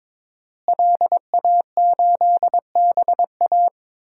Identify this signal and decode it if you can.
Morse code, heard as a keyed tone, about 22 wpm, 700 Hz. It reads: LA8BA
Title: DXCC-100 Call Signs - 5 Characters - Repeated 2x - 22wpm